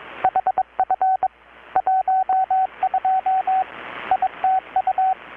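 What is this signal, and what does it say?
Morse code, heard as a keyed tone, about 22 wpm, 760 Hz, with radio noise. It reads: HF 12 ITU
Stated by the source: Morse from KPH's Night of Nights XX maritime radiotelegraphy event, received in Chiapas, Mexico about 3,800 km away on a Kenwood R-600 with a grounded 10 m long-wire antenna.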